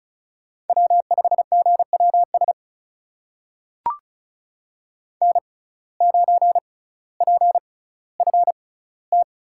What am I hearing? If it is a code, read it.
Morse code, heard as a keyed tone, about 36 wpm, spaced as for 18 wpm, 700 Hz. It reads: W5GWS N9PFT
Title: USA Call Signs - 5 Characters - 6x Character Spacing - 35wpm